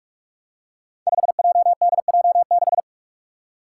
Morse code, heard as a keyed tone, about 44 wpm, 700 Hz. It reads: 5JDJ6
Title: Five Challenging Letter-Number Combinations 45wpm